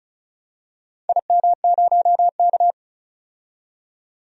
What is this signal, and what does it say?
Morse code, heard as a keyed tone, about 34 wpm, 700 Hz. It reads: IM0K
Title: DXCC-100 Call Signs - 4 Characters - 35wpm